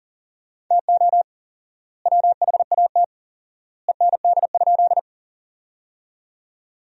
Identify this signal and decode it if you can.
Morse code, heard as a keyed tone, about 40 wpm, 700 Hz. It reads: TO WHAT END?
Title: 4x Word-Spacing - Sentences from Top 200 Words - 40wpm